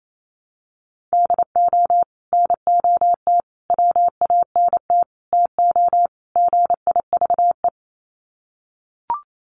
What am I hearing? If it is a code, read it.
Morse code, heard as a keyed tone, about 28 wpm, 700 Hz. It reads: DO NOT WANT TO GIVE
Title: Sets of 5 Words Encore 28wpm